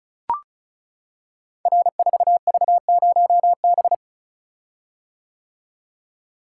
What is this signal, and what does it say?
Morse code, heard as a keyed tone, about 36 wpm, 700 Hz. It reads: R4V0B
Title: Code Groups - 5 Letter:Numbers - 35wpm